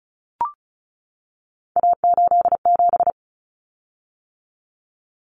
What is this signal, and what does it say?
Morse code, heard as a keyed tone, about 34 wpm, 700 Hz. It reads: A87
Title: Common Cut Numbers - 3-Digit 35wpm